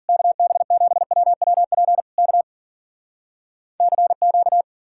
Morse code, heard as a keyed tone, about 54 wpm, 700 Hz. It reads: KB7WWP K CQ